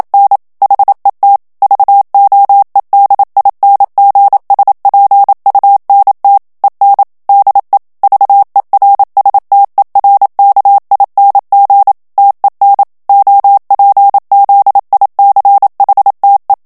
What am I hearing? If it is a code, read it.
Morse code, heard as a keyed tone, about 27 wpm, 800 Hz. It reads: N HET VOEDINGSPUNT EN DE VERSTERKING TEN OPZICHTE
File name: HOLLANDAIS texte clair 1 vitesse 26 mots minute